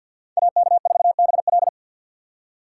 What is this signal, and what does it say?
Morse code, heard as a keyed tone, about 46 wpm, 700 Hz. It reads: AK4BL
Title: USA Call Signs - 2x Character Spacing - 50wpm